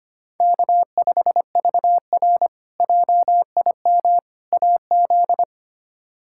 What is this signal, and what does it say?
Morse code, heard as a keyed tone, about 25 wpm, 700 Hz. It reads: K5VR JIM AZ
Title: K1USN SST QSO Practice - 25wpm